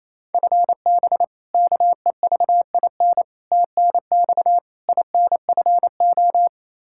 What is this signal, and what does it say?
Morse code, heard as a keyed tone, about 28 wpm, 700 Hz. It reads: FB KEVIN TNX INFO
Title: Practice QSO - Exchange 3 - More Info - 28wpm